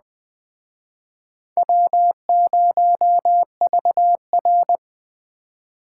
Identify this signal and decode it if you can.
Morse code, heard as a keyed tone, about 21 wpm, 700 Hz. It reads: W0VR
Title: Speed Racing - USA 1x2 Call Signs 17wpm